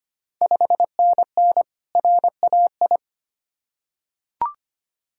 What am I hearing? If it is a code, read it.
Morse code, heard as a keyed tone, about 25 wpm, 700 Hz. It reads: 5NN RAI
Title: Ontario QSO Party Exchange 25wpm